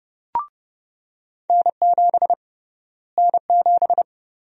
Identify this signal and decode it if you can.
Morse code, heard as a keyed tone, about 32 wpm, 700 Hz. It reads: N7 N7
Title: USA Call Sign Prefixes - Repeated 2x - 30wpm